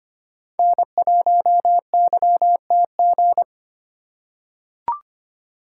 Morse code, heard as a keyed tone, about 25 wpm, 700 Hz. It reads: N1YTG
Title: USA Call Signs - 5 Characters - 3x Character Spacing - 25wpm